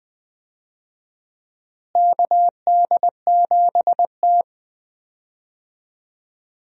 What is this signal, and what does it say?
Morse code, heard as a keyed tone, about 20 wpm, 700 Hz. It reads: KD7T
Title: USA Call Signs - 4 Characters - Repeated 2x - 20wpm